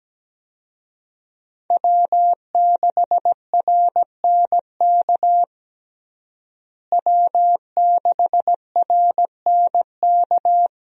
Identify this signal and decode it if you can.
Morse code, heard as a keyed tone, about 17 wpm, 700 Hz. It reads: W6RNK W6RNK
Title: USA Call Signs - 5 Characters - Repeated 2x - 17wpm